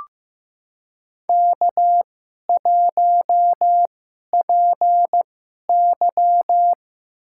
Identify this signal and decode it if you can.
Morse code, heard as a keyed tone, about 16 wpm, spaced as for 11 wpm, 700 Hz. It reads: K1PY
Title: USA Call Signs - 4 Characters - 2x Character Spacing - 15wpm